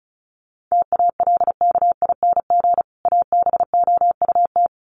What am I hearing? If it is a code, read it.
Morse code, heard as a keyed tone, about 34 wpm, 700 Hz. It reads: TALKING ABOUT
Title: Sets of 2 Words 35wpm